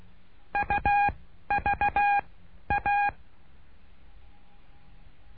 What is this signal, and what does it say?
Morse code, heard as a keyed tone, about 16 wpm, spaced as for 11 wpm, 810 Hz. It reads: UVA
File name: cursos / CURSO DE TELEGRAFIA / Telegrafia - Trilha 09